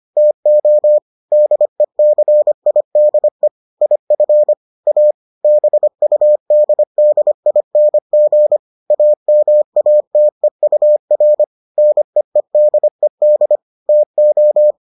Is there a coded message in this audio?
Yes: Morse code from 0.2 to 14.7 s, heard as a keyed tone, about 25 wpm, 600 Hz. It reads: TO DECIDE IF A BUDDING AMATEUR NEEDED TO